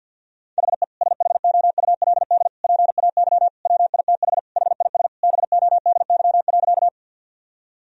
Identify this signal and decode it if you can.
Morse code, heard as a keyed tone, about 50 wpm, 700 Hz. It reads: HE SHOULD PAY WITH HIS BODY.
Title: Sentences from Top 1000 Words - Review Part 2 - 50wpm